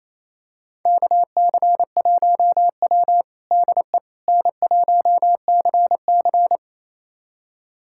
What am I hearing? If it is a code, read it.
Morse code, heard as a keyed tone, about 28 wpm, 700 Hz. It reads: KC1W DE N1CC
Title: USA Callsign DE Callsign - 4 Characters - 28wpm